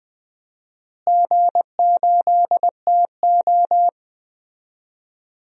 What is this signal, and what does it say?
Morse code, heard as a keyed tone, about 20 wpm, 700 Hz. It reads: G8TO
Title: DXCC-100 Call Signs - 4 Characters - Repeated 3x - 20wpm